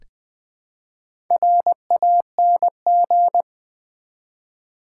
Morse code, heard as a keyed tone, about 20 wpm, 700 Hz. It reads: RANG
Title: Build A Word - 1 Letter At A Time - 5-Letter Words - 20wpm